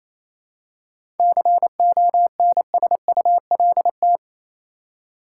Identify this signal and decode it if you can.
Morse code, heard as a keyed tone, about 27 wpm, 700 Hz. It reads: CONSULT
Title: Seven Letter Words - 28wpm